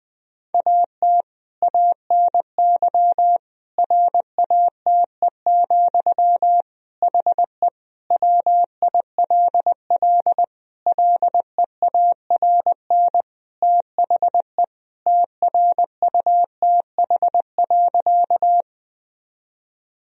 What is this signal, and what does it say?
Morse code, heard as a keyed tone, about 21 wpm, 700 Hz. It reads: AT ANY RATE, HE WILL LEARN THE TRUTH.
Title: Sentences from Top 1000 Words - Review Part 4 - 20wpm